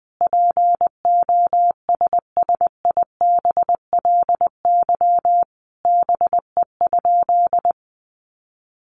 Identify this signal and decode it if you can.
Morse code, heard as a keyed tone, about 20 wpm, 700 Hz. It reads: POSSIBLY BE?
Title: Sentences from Top 1300 Words 20wpm